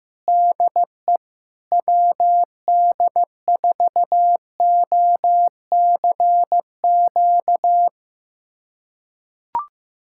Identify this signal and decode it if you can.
Morse code, heard as a keyed tone, about 15 wpm, 700 Hz. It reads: DE WD4OCQ
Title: USA Callsign DE Callsign 15wpm